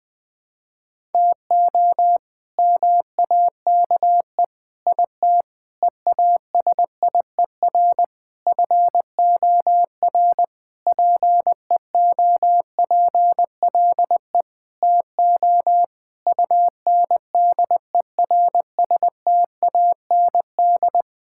Morse code, heard as a keyed tone, about 20 wpm, 700 Hz. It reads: TO MAKE IT EASIER FOR PEOPLE TO UNDERSTAND
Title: Sentences with 7-Word Phrases - 20wpm